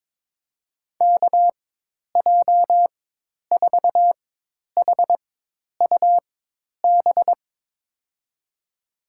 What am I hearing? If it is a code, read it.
Morse code, heard as a keyed tone, about 22 wpm, spaced as for 10 wpm, 700 Hz. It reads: KJ4HUB
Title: USA Call Signs - 4x Character Spacing - 22wpm